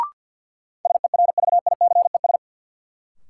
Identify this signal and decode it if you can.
Morse code, heard as a keyed tone, about 50 wpm, 700 Hz. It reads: SERVICES
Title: Sets of 5 Words - New Word Drill - 50wpm